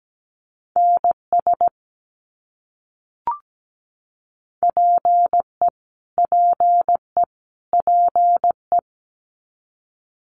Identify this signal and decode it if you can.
Morse code, heard as a keyed tone, about 17 wpm, 700 Hz. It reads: NS PE PE PE
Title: Canadian Provinces and Territories - Repeated 3x - 17wpm